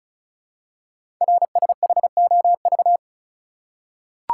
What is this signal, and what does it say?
Morse code, heard as a keyed tone, about 36 wpm, 700 Hz. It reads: RSHOV
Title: Code Groups - 5 Letter:Numbers - 35wpm